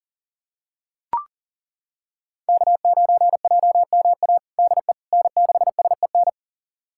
Morse code, heard as a keyed tone, about 40 wpm, 700 Hz. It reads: K9JMA DE N6SEN
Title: USA Callsign DE Callsign - 5 Characters - 40wpm